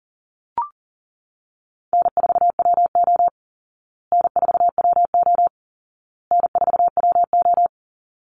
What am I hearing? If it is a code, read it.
Morse code, heard as a keyed tone, about 40 wpm, 700 Hz. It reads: N4WO N4WO N4WO
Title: Call Signs Repeated 2x - 40wpm